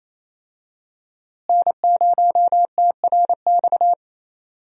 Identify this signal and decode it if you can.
Morse code, heard as a keyed tone, about 28 wpm, 700 Hz. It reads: N0TRX
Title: USA Call Signs - 5 Characters - Repeated 2x - 28wpm